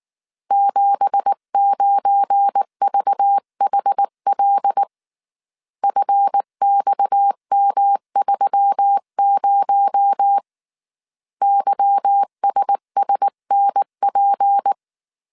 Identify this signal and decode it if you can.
Morse code, heard as a keyed tone, about 19 wpm, 800 Hz. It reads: 79VHL FXM30 YSSNP